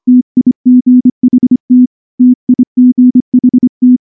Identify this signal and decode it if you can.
Morse code, heard as a keyed tone, about 24 wpm, 250 Hz. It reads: TIGHT TIGHT
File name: morse / morse-training-25WPM